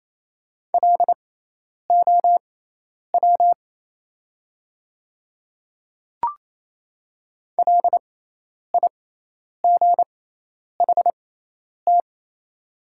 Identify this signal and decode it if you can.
Morse code, heard as a keyed tone, about 27 wpm, spaced as for 10 wpm, 700 Hz. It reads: LOW LIGHT
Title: 6x Character Spacing - Top 200 Words - 28wpm